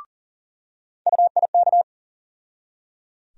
Morse code, heard as a keyed tone, about 37 wpm, 700 Hz. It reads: UIK
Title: USA Call Sign Postfixes - 3 Letters 40wpm